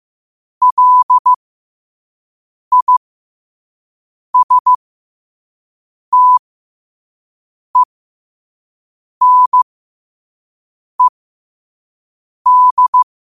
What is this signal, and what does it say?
Morse code, heard as a keyed tone, about 15 wpm, spaced as for 5 wpm, 1000 Hz. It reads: LISTENED
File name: qotdfiles / 5wpm